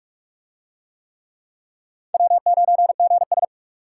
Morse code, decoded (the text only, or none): W9GS